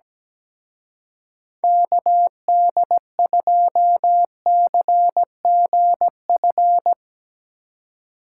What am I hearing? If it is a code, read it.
Morse code, heard as a keyed tone, about 19 wpm, 700 Hz. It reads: KD2CGF
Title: Speed Racing - USA 2x3 Call Signs 17wpm